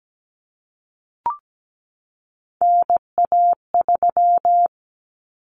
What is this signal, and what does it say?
Morse code, heard as a keyed tone, about 17 wpm, 700 Hz. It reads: NA3
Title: Common Cut Numbers - 3-Digit 17wpm